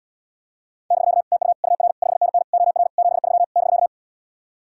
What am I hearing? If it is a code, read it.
Morse code, heard as a keyed tone, about 79 wpm, 700 Hz. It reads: <BK> TU UR 5NN OR 73 <BK>